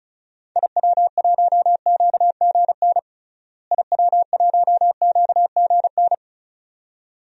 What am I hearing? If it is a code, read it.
Morse code, heard as a keyed tone, about 34 wpm, 700 Hz. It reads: IW1QGN IW1QGN
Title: DXCC-100 Call Signs - 6 Characters - Repeated 2x - 35wpm